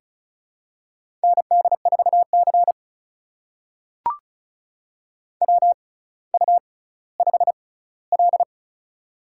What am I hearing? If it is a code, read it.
Morse code, heard as a keyed tone, about 36 wpm, spaced as for 18 wpm, 700 Hz. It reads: ND4C WU5L
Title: USA Call Signs - 4 Characters - 6x Character Spacing - 35wpm